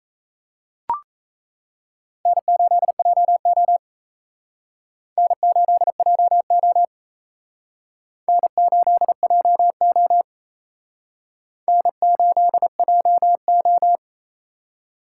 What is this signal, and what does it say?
Morse code, heard as a keyed tone, about 35 wpm, 700 Hz. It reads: N8JO N8JO N8JO N8JO
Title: Speed Racing - USA 1x2 Call Signs 28wpm